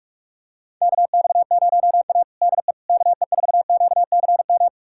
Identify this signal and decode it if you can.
Morse code, heard as a keyed tone, about 45 wpm, 690 Hz. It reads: KX0A DE KE4QCM
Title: USA Callsign DE Callsign 45wpm